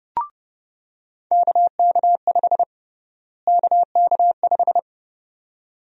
Morse code, heard as a keyed tone, about 30 wpm, 700 Hz. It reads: KK5 KK5
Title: USA Call Sign Prefixes - Repeated 2x - 30wpm